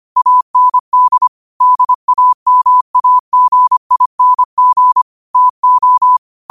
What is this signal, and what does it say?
Morse code, heard as a keyed tone, about 25 wpm, 1000 Hz. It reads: AND DAMAGING TO
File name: qotdfiles / 25wpm